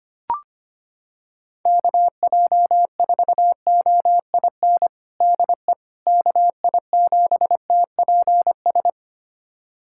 Morse code, heard as a keyed tone, about 25 wpm, 700 Hz. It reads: KJ4OIN DE KI7TPS
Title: USA Callsign DE Callsign 25wpm